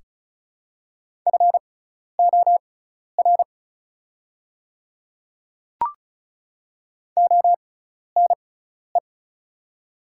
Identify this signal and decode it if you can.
Morse code, heard as a keyed tone, about 36 wpm, spaced as for 12 wpm, 700 Hz. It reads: FOR ONE
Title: 6x Character Spacing - Top 100 Words - 35wpm